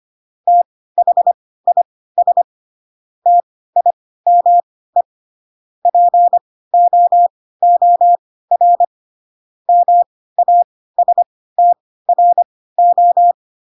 Morse code, heard as a keyed tone, about 25 wpm, spaced as for 16 wpm, 700 Hz. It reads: THIS TIME POOR MASTRO
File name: Pinocchio / Pinocchio ch1 10 25 ik1hge